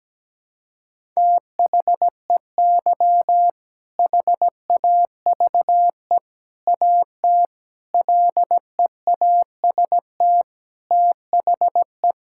Morse code, heard as a keyed tone, about 17 wpm, 700 Hz. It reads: THEY HAVE AT LEAST THE
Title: Sentences from Top 1000 Words - Review Part 6 - 17wpm